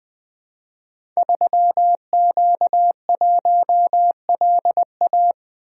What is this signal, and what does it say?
Morse code, heard as a keyed tone, about 20 wpm, 700 Hz. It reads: 3Q1LA
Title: Code Groups - 5 Letter:Numbers - 20wpm